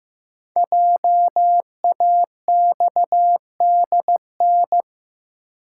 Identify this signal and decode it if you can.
Morse code, heard as a keyed tone, about 15 wpm, 700 Hz. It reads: JAXDN
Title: Code Groups - 5 Letters - 15wpm